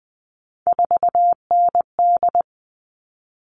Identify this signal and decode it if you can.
Morse code, heard as a keyed tone, about 20 wpm, 700 Hz. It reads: 4ND
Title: All Cut Numbers - 3-Digit 20wpm